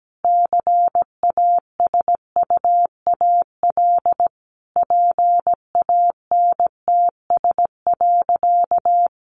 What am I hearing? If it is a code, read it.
Morse code, heard as a keyed tone, about 17 wpm, 700 Hz. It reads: CASUAL PANTS.
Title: Additional Easy Phrases 17wpm